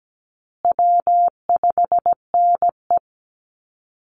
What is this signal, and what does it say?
Morse code, heard as a keyed tone, about 17 wpm, 700 Hz. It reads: W5NE
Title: Call Signs Repeated 2x - 17wpm